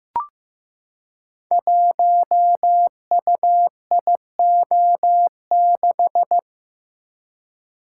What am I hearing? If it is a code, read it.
Morse code, heard as a keyed tone, about 15 wpm, 700 Hz. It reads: 1UIO6
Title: Code Groups - 5 Letter:Numbers - 15wpm